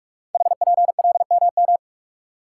Morse code, heard as a keyed tone, about 44 wpm, 690 Hz. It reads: HPLMM
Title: Code Groups - 5 Letters - 45wpm